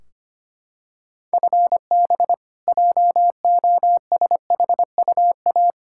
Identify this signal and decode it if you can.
Morse code, heard as a keyed tone, about 25 wpm, 700 Hz. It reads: FB JOSHUA
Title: K1USN SST QSO Practice 25wpm